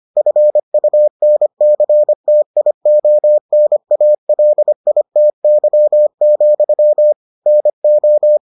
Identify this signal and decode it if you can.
Morse code, heard as a keyed tone, about 25 wpm, 600 Hz. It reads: FUNCTIONALITY, NO